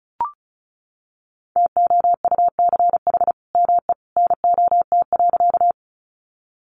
Morse code, heard as a keyed tone, about 34 wpm, 700 Hz. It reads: TOUCH ME NOT.
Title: Sentences from Top 1200 Words 35wpm